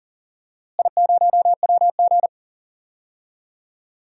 Morse code, heard as a keyed tone, about 39 wpm, 700 Hz. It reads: I0WG
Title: DXCC-100 Call Signs - 4 Characters - Repeated 2x - 40wpm